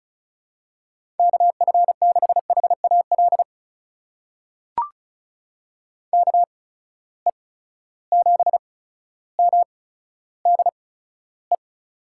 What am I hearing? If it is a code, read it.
Morse code, heard as a keyed tone, about 36 wpm, spaced as for 15 wpm, 700 Hz. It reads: KF6HAL KE7MDE
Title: USA Call Signs - 6 Characters - 8x Character Spacing - 35wpm